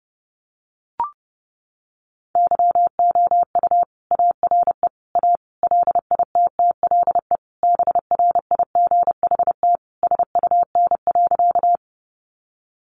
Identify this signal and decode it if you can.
Morse code, heard as a keyed tone, about 30 wpm, 700 Hz. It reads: YOU ARE A LITTLE BRIGHT SUN.
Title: Sentences from Top 1000 Words 30wpm